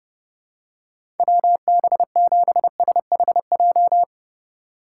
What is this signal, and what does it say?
Morse code, heard as a keyed tone, about 31 wpm, 700 Hz. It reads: WB7SHJ
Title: USA Call Signs - 6 Characters - 3x Character Spacing - 30wpm